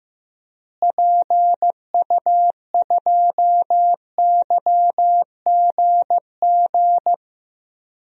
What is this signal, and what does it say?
Morse code, heard as a keyed tone, about 16 wpm, 700 Hz. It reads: PU2YGG
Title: DXCC-100 Call Signs - 6 Characters - Repeated 3x - 15wpm